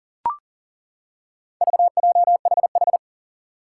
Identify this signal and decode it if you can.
Morse code, heard as a keyed tone, about 40 wpm, 700 Hz. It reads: VJHH
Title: Four Challenging Letter-Number Combinations 40wpm